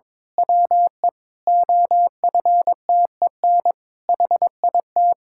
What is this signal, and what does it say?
Morse code, heard as a keyed tone, about 21 wpm, 700 Hz. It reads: WE OFTEN HIT
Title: Pangram Challenge v2 - 22wpm